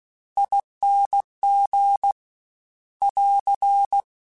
Morse code, heard as a keyed tone, about 16 wpm, 800 Hz. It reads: ING <AR>
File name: Special INGs / Special INGs-mp3s / Ere ing @15